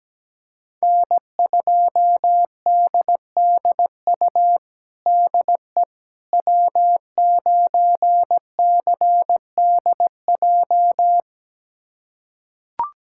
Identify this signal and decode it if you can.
Morse code, heard as a keyed tone, about 17 wpm, 700 Hz. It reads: N2DDU DE W9CDJ
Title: USA Callsign DE Callsign - 5 Characters - 17wpm